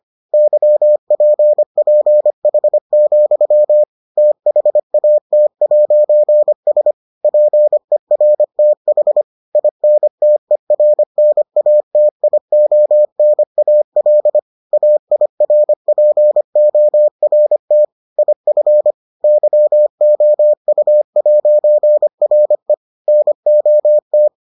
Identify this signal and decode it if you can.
Morse code, heard as a keyed tone, about 25 wpm, 600 Hz. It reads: YPPH, THAT'S PERTH INTERNATIONAL AIRPORT IF YOU'RE NOT